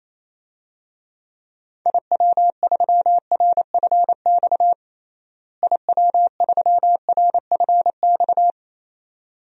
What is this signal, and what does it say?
Morse code, heard as a keyed tone, about 27 wpm, 700 Hz. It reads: IW3RFX IW3RFX
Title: DXCC-100 Call Signs - 6 Characters - Repeated 2x - 28wpm